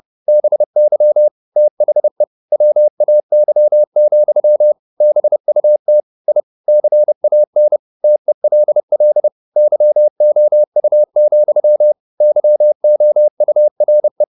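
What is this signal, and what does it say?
Morse code, heard as a keyed tone, about 30 wpm, 600 Hz. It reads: BY THE WAY, BUT I CAN TELL YOU, YOURE
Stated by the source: Morse code at 30 wpm, the novel Never by Ken Follett in English, from CW med Gnister.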